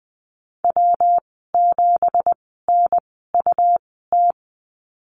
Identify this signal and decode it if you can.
Morse code, heard as a keyed tone, about 20 wpm, spaced as for 15 wpm, 700 Hz. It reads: W7NUT
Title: 186 Lesson 30g - Letter b - Call Signs 20wpm